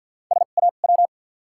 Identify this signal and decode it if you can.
Morse code, heard as a keyed tone, about 48 wpm, 700 Hz. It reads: S A W
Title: 2x Character Spacing - Top 500 Words - 50wpm